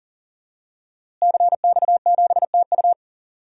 Morse code, heard as a keyed tone, about 40 wpm, 700 Hz. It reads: CX7TU
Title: DXCC-100 Call Signs - 5 Characters - Repeated 3x - 40wpm